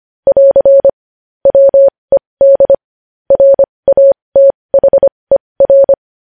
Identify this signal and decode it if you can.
Morse code, heard as a keyed tone, about 25 wpm, spaced as for 20 wpm, 550 Hz. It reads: <AR> WED RATHER
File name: podcast / News headlines-1x-25x20